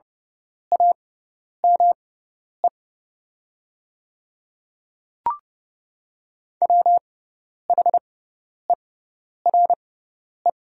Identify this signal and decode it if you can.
Morse code, heard as a keyed tone, about 30 wpm, spaced as for 10 wpm, 690 Hz. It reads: AME WHERE
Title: 6x Character Spacing - Top 200 Words - 30wpm